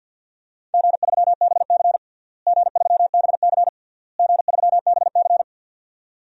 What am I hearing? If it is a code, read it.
Morse code, heard as a keyed tone, about 50 wpm, 700 Hz. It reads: G3BC G3BC G3BC
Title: DXCC-100 Call Signs - 4 Characters - Repeated 3x - 50wpm